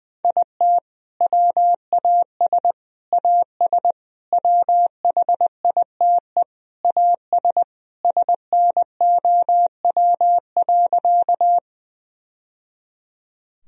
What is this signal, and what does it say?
Morse code, heard as a keyed tone, about 20 wpm, 700 Hz. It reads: IT WAS AS WHITE AS SNOW.
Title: Sentences from Top 1000 Words - Review Part 3 - 20wpm